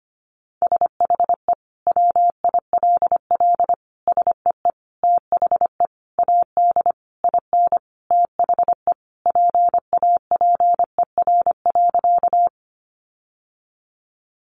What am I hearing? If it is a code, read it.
Morse code, heard as a keyed tone, about 25 wpm, 700 Hz. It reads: SHE WILL SEE THE AD IN THE PAPER.
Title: Sentences from Top 1300 Words 25wpm